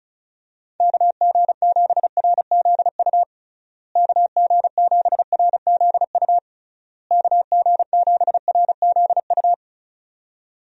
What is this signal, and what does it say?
Morse code, heard as a keyed tone, about 34 wpm, 700 Hz. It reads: KG7RZU KG7RZU KG7RZU
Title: USA Call Signs - 6 Characters - Repeated 3x - 35wpm